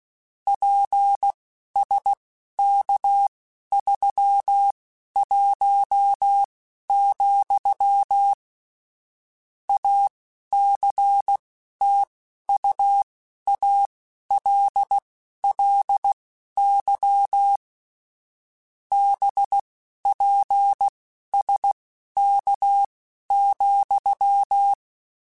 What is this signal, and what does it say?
Morse code, heard as a keyed tone, about 16 wpm, spaced as for 11 wpm, 800 Hz. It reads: PSK31, ACTUALLY BPSK,